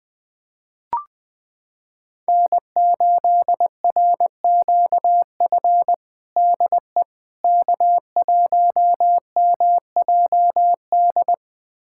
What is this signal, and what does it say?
Morse code, heard as a keyed tone, about 21 wpm, 700 Hz. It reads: N8RQF DE K1MJD
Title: USA Callsign DE Callsign - 5 Characters - 20wpm